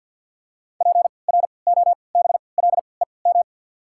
Morse code, heard as a keyed tone, about 53 wpm, spaced as for 29 wpm, 690 Hz. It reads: PROBLEM
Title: 3x Character Spacing - Top 500 Words - 50wpm